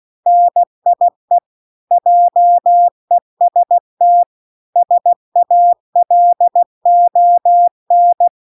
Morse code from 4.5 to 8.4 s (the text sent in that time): SALON